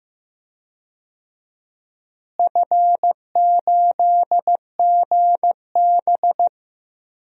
Morse code, heard as a keyed tone, about 15 wpm, 700 Hz. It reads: F8GB
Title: DXCC-100 Call Signs - 4 Characters - Repeated 3x - 15wpm